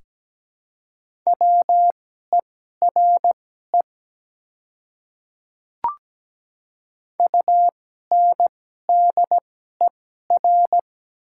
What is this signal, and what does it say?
Morse code, heard as a keyed tone, about 17 wpm, spaced as for 12 wpm, 700 Hz. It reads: WERE UNDER
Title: 2x Character Spacing - Top 200 Words - 17wpm